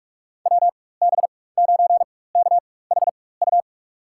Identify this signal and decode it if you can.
Morse code, heard as a keyed tone, about 46 wpm, spaced as for 21 wpm, 700 Hz. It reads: WB9KHU
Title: USA Call Signs - 6 Characters - 4x Character Spacing - 45wpm